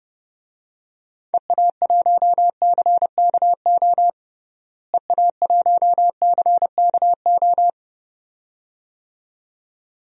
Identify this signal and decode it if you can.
Morse code, heard as a keyed tone, about 29 wpm, 700 Hz. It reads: EA1CKO EA1CKO
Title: DXCC-100 Call Signs - 6 Characters - Repeated 2x - 30wpm